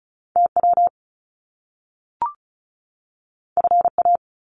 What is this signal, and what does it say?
Morse code, heard as a keyed tone, about 36 wpm, 700 Hz. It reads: TW FA
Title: Top Two Letter Combinations in Words 35wpm